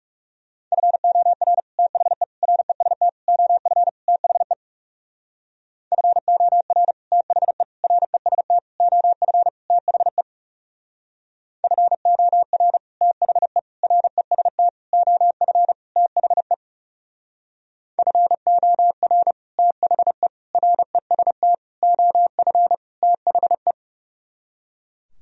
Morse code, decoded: FOR THE REST OF THE FOR THE REST OF THE FOR THE REST OF THE FOR THE REST OF THE

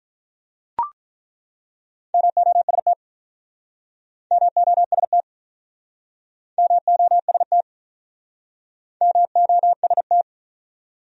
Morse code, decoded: MOST MOST MOST MOST

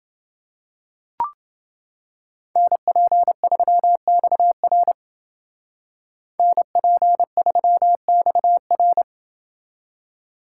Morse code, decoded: NP3XR NP3XR

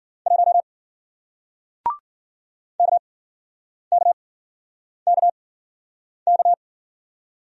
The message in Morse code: 1 KKKK

